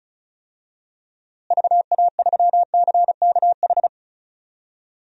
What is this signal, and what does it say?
Morse code, heard as a keyed tone, about 34 wpm, 700 Hz. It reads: VA3CKH
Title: DXCC-100 Call Signs - 6 Characters - Repeated 3x - 35wpm